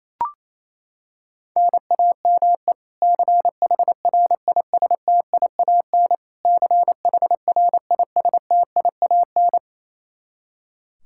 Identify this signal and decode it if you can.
Morse code, heard as a keyed tone, about 28 wpm, 700 Hz. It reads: NAME CHRISTIAN CHRISTIAN
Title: Practice QSO - Exchange 1 - Beginning - 28wpm